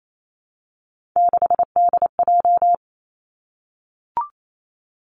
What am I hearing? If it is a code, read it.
Morse code, heard as a keyed tone, about 28 wpm, 700 Hz. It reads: 6DJ
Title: Three Challenging Letter-Number Combinations 28wpm